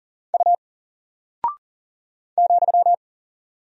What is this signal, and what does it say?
Morse code, heard as a keyed tone, about 40 wpm, 700 Hz. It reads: U ,
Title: Single Character (Letter:Number:Punctuation:Prosign) Alt 40wpm